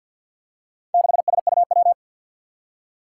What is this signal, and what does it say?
Morse code, heard as a keyed tone, about 53 wpm, 700 Hz. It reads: 6SUW
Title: Four Challenging Letter-Number Combinations 50wpm